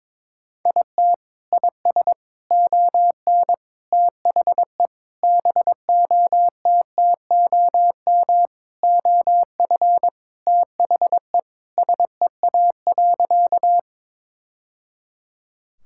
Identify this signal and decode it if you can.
Morse code, heard as a keyed tone, about 22 wpm, 700 Hz. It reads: IT IS ON THE BOTTOM OF THE SEA.
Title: Sentences from Top 1000 Words - Review Part 6 - 22wpm